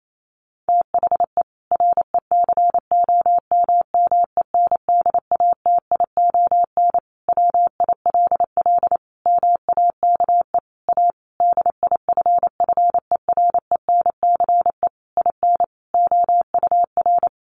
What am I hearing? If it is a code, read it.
Morse code, heard as a keyed tone, about 28 wpm, 700 Hz. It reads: THE RECOMMENDATION WILL MAKE A DIFFERENCE IN OUR
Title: Sentences from Top 1400 Words 28wpm